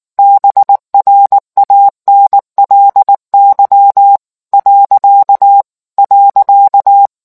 Morse code, heard as a keyed tone, about 19 wpm, 800 Hz. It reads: BRANLY .